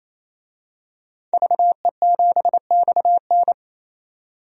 Morse code, heard as a keyed tone, about 28 wpm, 700 Hz. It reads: VE7XN
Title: DXCC-100 Call Signs - 5 Characters - 28wpm